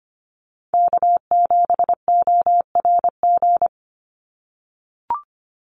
Morse code, heard as a keyed tone, about 25 wpm, 700 Hz. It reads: K7ORG
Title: NA SOTA Chaser Callsigns 25wpm